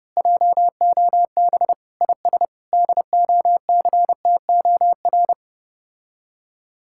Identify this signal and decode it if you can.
Morse code, heard as a keyed tone, about 29 wpm, 700 Hz. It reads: JOB IS DOCTOR